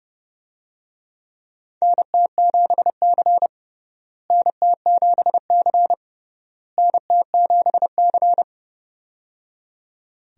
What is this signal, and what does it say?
Morse code, heard as a keyed tone, about 30 wpm, 700 Hz. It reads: NT7C NT7C NT7C
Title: USA Call Signs - 4 Characters - Repeated 3x - 30wpm